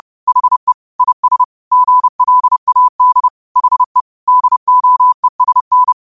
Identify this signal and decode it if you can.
Morse code, heard as a keyed tone, about 30 wpm, 1000 Hz. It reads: HE IS GLAD HE DOESN
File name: qotdfiles / 30wpm